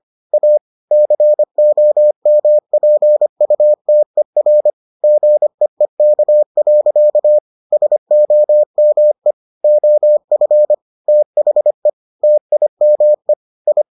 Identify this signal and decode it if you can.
Morse code, heard as a keyed tone, about 25 wpm, 600 Hz. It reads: A COMPUTER GEEK. SOME OF THE TIME I